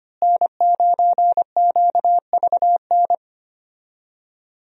Morse code, heard as a keyed tone, about 25 wpm, 700 Hz. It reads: N9QVN
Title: USA Call Signs - 5 Characters - Repeated 2x - 25wpm